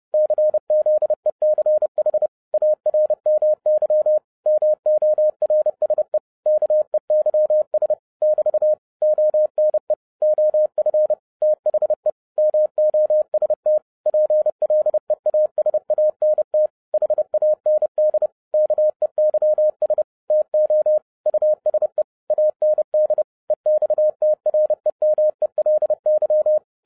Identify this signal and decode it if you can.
Morse code, heard as a keyed tone, about 30 wpm, 610 Hz. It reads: CZECH ARMY MORSE KEYS <BT> ONE OF THE MOST PLEASANT HAND KEYS TO USE AND EXTREMELY
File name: 2011-10-mp3 / n10-Czech Army Morse Keys 30wpm